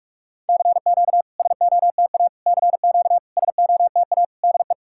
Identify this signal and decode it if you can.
Morse code, heard as a keyed tone, about 46 wpm, 690 Hz. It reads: CQ SOTA CQ SOTA DE
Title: SOTA QSO Practice 45wpm